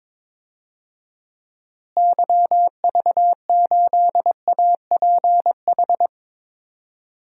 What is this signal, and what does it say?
Morse code, heard as a keyed tone, about 22 wpm, 700 Hz. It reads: YV8APH
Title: DXCC-100 Call Signs - 6 Characters - Repeated 3x - 22wpm